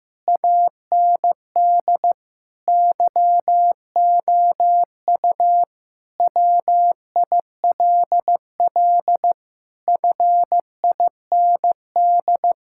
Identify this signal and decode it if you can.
Morse code, heard as a keyed tone, about 15 wpm, 700 Hz. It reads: AND YOU WILL FIND
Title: Sentences from Top 1000 Words - Review Part 6 - 15wpm